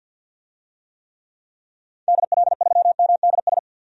Morse code, heard as a keyed tone, about 52 wpm, 700 Hz. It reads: DL3MDS
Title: DXCC-100 Call Signs - 6 Characters - 50wpm